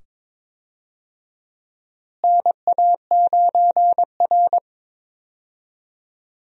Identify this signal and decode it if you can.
Morse code, heard as a keyed tone, about 22 wpm, 700 Hz. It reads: NA9R
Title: USA Call Signs - 4 Characters - Repeated 2x - 22wpm